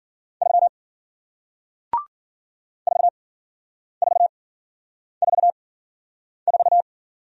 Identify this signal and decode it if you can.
Morse code, heard as a keyed tone, about 54 wpm, spaced as for 9 wpm, 700 Hz. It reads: 3 4444